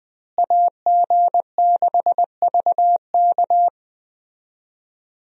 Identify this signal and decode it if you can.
Morse code, heard as a keyed tone, about 20 wpm, 700 Hz. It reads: AG6VK
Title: USA Call Signs - 5 Characters - Repeated 3x - 20wpm